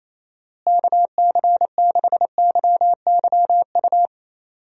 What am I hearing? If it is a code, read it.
Morse code, heard as a keyed tone, about 27 wpm, 700 Hz. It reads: KC6YYU
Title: USA Call Signs - 6 Characters - Repeated 2x - 28wpm